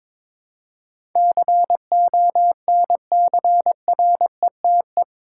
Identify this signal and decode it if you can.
Morse code, heard as a keyed tone, about 21 wpm, 700 Hz. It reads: CONCRETE
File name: practice / 22wpm / Eight Letter Words - 22wpm